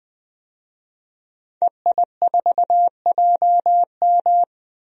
Morse code, heard as a keyed tone, about 20 wpm, 700 Hz. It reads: EI4JM
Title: DXCC-100 Call Signs - 5 Characters - 20wpm